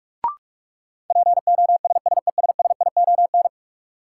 Morse code, heard as a keyed tone, about 46 wpm, 700 Hz. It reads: POSSESSION
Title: Sentences from Top 1400 Words - New Word Drill Alt - 45wpm